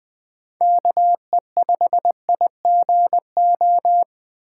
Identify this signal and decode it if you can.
Morse code, heard as a keyed tone, about 20 wpm, 700 Hz. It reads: KE5IGO
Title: USA Call Signs - 6 Characters - 4x Character Spacing - 20wpm